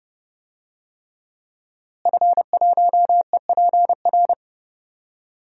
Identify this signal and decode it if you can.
Morse code, heard as a keyed tone, about 30 wpm, 700 Hz. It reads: F1EPR
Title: DXCC-100 Call Signs - 5 Characters - Repeated 3x - 30wpm